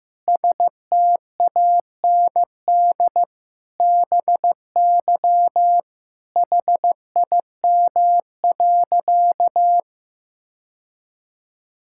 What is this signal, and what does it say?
Morse code, heard as a keyed tone, about 15 wpm, 700 Hz. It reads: STAND BY HIM.
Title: Sentences from Top 1000 Words - Review Part 5 - 15wpm